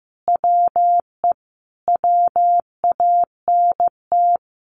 Morse code, heard as a keyed tone, about 15 wpm, 700 Hz. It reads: WE WANT
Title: Sentences from Top 1400 Words 15wpm